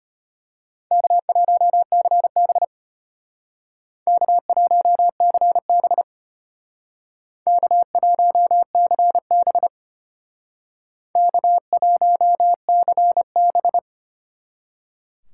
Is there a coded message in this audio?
Yes: Morse code from 0.9 to 13.8 s, heard as a keyed tone, about 33 wpm, 700 Hz. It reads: K1CB K1CB K1CB K1CB